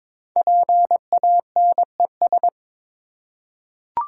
Morse code, heard as a keyed tone, about 22 wpm, 700 Hz. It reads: PANES
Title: Code Groups - 5 Letters - 22wpm